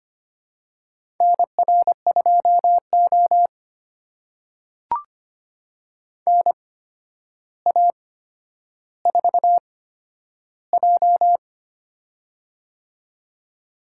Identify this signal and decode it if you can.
Morse code, heard as a keyed tone, about 25 wpm, spaced as for 11 wpm, 700 Hz. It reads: NR2O NA4J
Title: USA Call Signs - 4 Characters - 8x Character Spacing - 25wpm